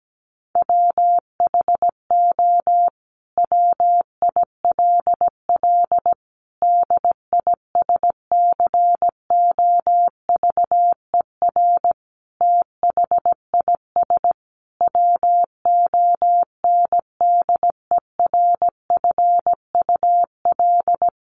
Morse code, heard as a keyed tone, about 17 wpm, 700 Hz. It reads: WHO WILL DISCOVER THIS WONDERFUL
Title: Sentences from Top 1100 Words 17wpm